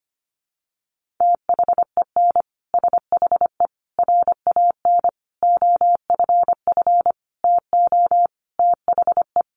Morse code, heard as a keyed tone, about 25 wpm, 700 Hz. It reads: THEN SHE RAN OFF TO THE
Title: Sentences from Top 1400 Words 25wpm